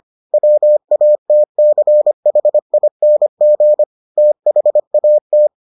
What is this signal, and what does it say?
Morse code, heard as a keyed tone, about 25 wpm, 600 Hz. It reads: WATCHING THAT